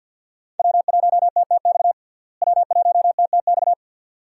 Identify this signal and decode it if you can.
Morse code, heard as a keyed tone, about 52 wpm, 700 Hz. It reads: W1TTX W1TTX
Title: USA Call Signs - 5 Characters - Repeated 2x - 50wpm